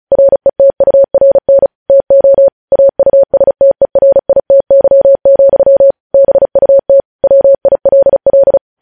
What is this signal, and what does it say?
Morse code, heard as a keyed tone, about 36 wpm, 550 Hz. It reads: RETURN TO AUSTERITY, BUT WILL
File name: podcast / News headlines-1x-35x35